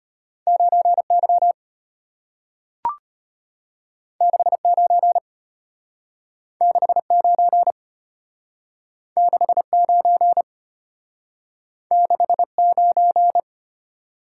Morse code, heard as a keyed tone, about 34 wpm, 700 Hz. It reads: 9Y 69 69 69 69